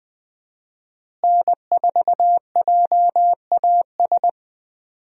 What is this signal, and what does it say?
Morse code, heard as a keyed tone, about 20 wpm, 700 Hz. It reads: N4JAS
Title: USA Call Signs - 5 Characters - Repeated 3x - 20wpm